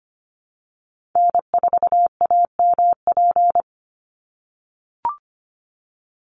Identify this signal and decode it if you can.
Morse code, heard as a keyed tone, about 25 wpm, 700 Hz. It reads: N4AMP
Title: USA Call Signs 25wpm